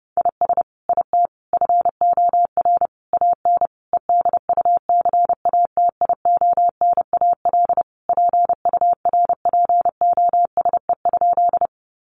Morse code, heard as a keyed tone, about 30 wpm, 700 Hz. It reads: IS IT FOR AN EDUCATIONAL PURPOSE?
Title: Sentences from Top 1200 Words 30wpm